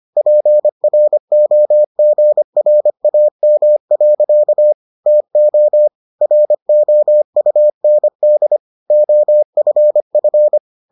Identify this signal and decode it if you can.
Morse code, heard as a keyed tone, about 25 wpm, 600 Hz. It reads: PROGRAM. TO ROUND OFF